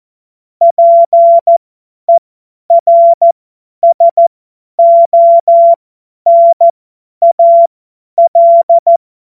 Morse code, heard as a keyed tone, about 14 wpm, spaced as for 10 wpm, 680 Hz. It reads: PERSONAL
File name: Morsekurs / Witze / Witz 16 Januar 2024 50 F70